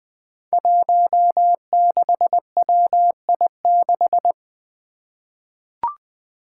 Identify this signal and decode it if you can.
Morse code, heard as a keyed tone, about 20 wpm, 700 Hz. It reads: 16WI6
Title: Five Challenging Letter-Number Combinations 20wpm